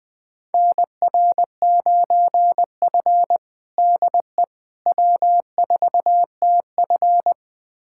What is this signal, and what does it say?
Morse code, heard as a keyed tone, about 20 wpm, 700 Hz. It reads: NR9F DE W4TF
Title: USA Callsign DE Callsign - 4 Characters - 20wpm